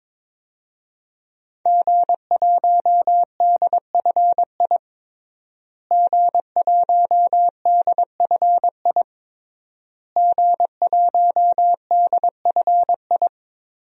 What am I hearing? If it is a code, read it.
Morse code, heard as a keyed tone, about 22 wpm, 700 Hz. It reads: G1DFI G1DFI G1DFI
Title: DXCC-100 Call Signs - 5 Characters - Repeated 3x - 22wpm